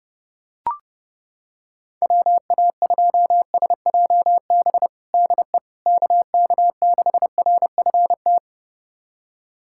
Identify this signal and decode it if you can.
Morse code, heard as a keyed tone, about 30 wpm, 700 Hz. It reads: WA2SJB DE KK6RFT
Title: USA Callsign DE Callsign - 6 Characters - 30wpm